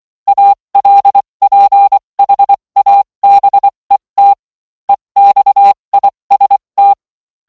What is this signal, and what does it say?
Morse code, heard as a keyed tone, about 23 wpm, spaced as for 20 wpm, 800 Hz. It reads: ALPHABET EXIST